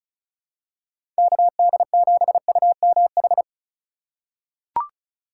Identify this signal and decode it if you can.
Morse code, heard as a keyed tone, about 34 wpm, 700 Hz. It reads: KD7UMH
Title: USA Call Signs - 6 Characters - 3x Character Spacing - 35wpm